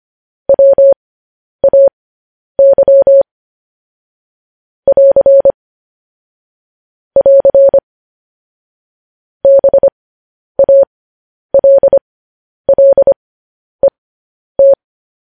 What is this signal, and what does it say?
Morse code, heard as a keyed tone, about 25 wpm, spaced as for 10 wpm, 550 Hz. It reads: WAY <AR> <AR> BALLET
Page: news headlines in morse code at 10 wpm